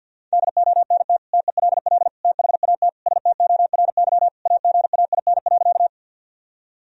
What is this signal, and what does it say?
Morse code, heard as a keyed tone, about 46 wpm, 700 Hz. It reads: DONT TELL THAT STORY AGAIN.
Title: Sentences from Top 1000 Words - Review Part 1 - 50wpm